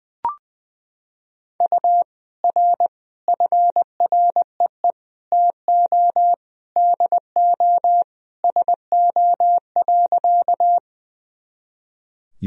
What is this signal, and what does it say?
Morse code, heard as a keyed tone, about 21 wpm, 700 Hz. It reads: U R FREE TO DO SO.
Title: Sentences from Top 500 Words and Abbreviations 20wpm